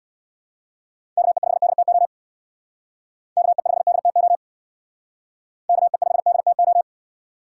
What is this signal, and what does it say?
Morse code, heard as a keyed tone, about 61 wpm, 700 Hz. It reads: KE5DTO KE5DTO KE5DTO
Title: Speed Racing - USA 2x3 Call Signs 50wpm